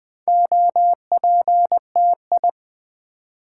Morse code, heard as a keyed tone, about 20 wpm, 700 Hz. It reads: OPTI
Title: Build A Word - 2 Letters At A Time - 8-Letter Words - 20wpm